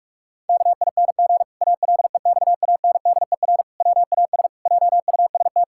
Morse code, heard as a keyed tone, about 46 wpm, 700 Hz. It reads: KING ALEXANDER WAS JUST
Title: Pangram Challenge v2 - 45wpm